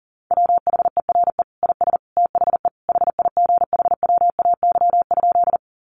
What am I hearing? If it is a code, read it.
Morse code, heard as a keyed tone, about 40 wpm, 700 Hz. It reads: WHERE IS THE HIGHWAY?
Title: Sentences from Top 1400 Words 40wpm